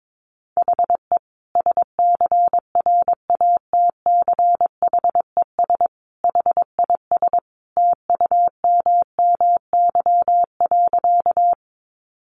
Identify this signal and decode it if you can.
Morse code, heard as a keyed tone, about 22 wpm, 700 Hz. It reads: HE SCRATCHES HIS TUMMY.
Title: Easy Phrases 22wpm